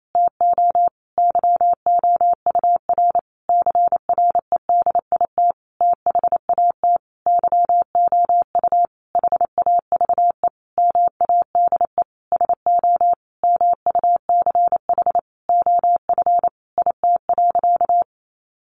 Morse code, heard as a keyed tone, about 28 wpm, 700 Hz. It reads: TO YOUR CREDIT THAT YOU HAVE MADE SO MUCH OF IT.
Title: Sentences from Top 1400 Words 28wpm